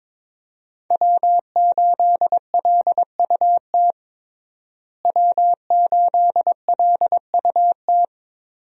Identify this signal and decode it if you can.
Morse code, heard as a keyed tone, about 22 wpm, 700 Hz. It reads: W8LUT W8LUT
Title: USA Call Signs - 5 Characters - Repeated 3x - 22wpm